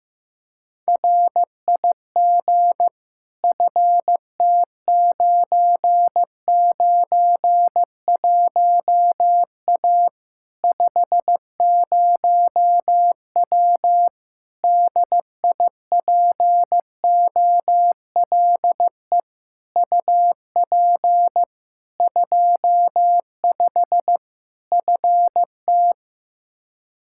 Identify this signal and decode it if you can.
Morse code, heard as a keyed tone, about 15 wpm, 700 Hz. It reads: RIG FT991A 50W DIPOLE UP 25 FT
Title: Practice QSO - Exchange 2 - Info - 15wpm